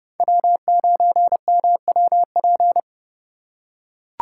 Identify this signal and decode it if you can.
Morse code, heard as a keyed tone, about 30 wpm, 700 Hz. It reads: W9MWP